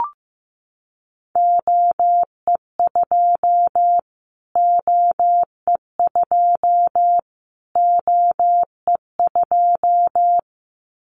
Morse code, heard as a keyed tone, about 15 wpm, 700 Hz. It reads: OE2 OE2 OE2
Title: Callsign Prefixes for Top 25 DXCC - Repeated 3x - 15wpm